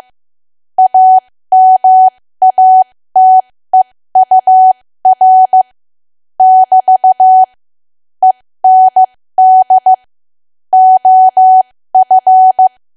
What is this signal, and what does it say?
Morse code, heard as a keyed tone, about 15 wpm, spaced as for 13 wpm, 750 Hz. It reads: AMATEUR <BT> END OF
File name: cwa / W1AW-Level2 / 13WPM